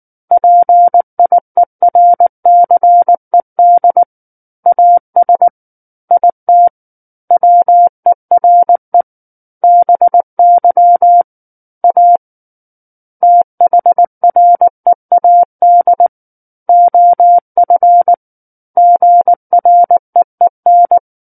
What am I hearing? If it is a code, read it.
Morse code, heard as a keyed tone, about 19 wpm, 700 Hz. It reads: PIERCED AS IT WERE BY A THREAD OF GREEN
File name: Book One / 10-Chapter 10 - 19 WPM